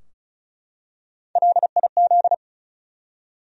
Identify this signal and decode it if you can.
Morse code, heard as a keyed tone, about 36 wpm, 700 Hz. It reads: LIZ
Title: USA Common Names - Max 4 Letters - 35wpm